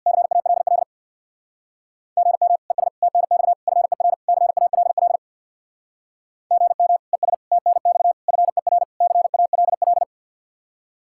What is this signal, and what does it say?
Morse code, heard as a keyed tone, about 64 wpm, 700 Hz. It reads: CALL GM ES TNX FER CALL GM ES TNX FER CALL